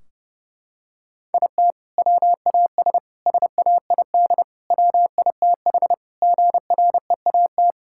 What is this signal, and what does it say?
Morse code, heard as a keyed tone, about 30 wpm, 700 Hz. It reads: IT WAS SAID WITH GREAT
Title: Sentences from Top 1600 Words 30wpm